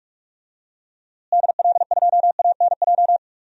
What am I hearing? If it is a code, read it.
Morse code, heard as a keyed tone, about 44 wpm, 700 Hz. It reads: DL2ANJ
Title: DXCC-100 Call Signs - 6 Characters - Repeated 3x - 45wpm